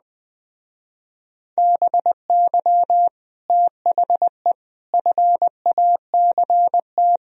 Morse code, heard as a keyed tone, about 20 wpm, 700 Hz. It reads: BY THE FACT
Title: Speed-Racing - Sets of 4 Words 17wpm